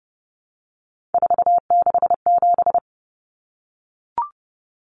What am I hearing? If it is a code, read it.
Morse code, heard as a keyed tone, about 30 wpm, 700 Hz. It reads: 467
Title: Numbers - 3-Digit 30wpm